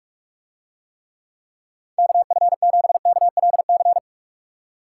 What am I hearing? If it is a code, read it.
Morse code, heard as a keyed tone, about 47 wpm, 700 Hz. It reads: KF7KLC